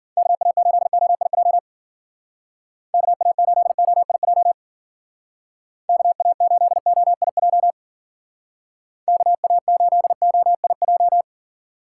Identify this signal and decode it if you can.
Morse code, heard as a keyed tone, about 50 wpm, 700 Hz. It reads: KA8OIJ KA8OIJ KA8OIJ KA8OIJ
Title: Speed Racing - USA 2x3 Call Signs 40wpm